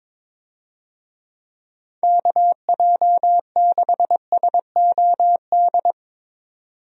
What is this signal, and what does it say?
Morse code, heard as a keyed tone, about 21 wpm, 700 Hz. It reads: KJ6SOD
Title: USA Call Signs - 6 Characters - 22wpm